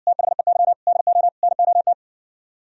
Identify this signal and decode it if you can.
Morse code, heard as a keyed tone, about 59 wpm, 700 Hz. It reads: THEY DO NOT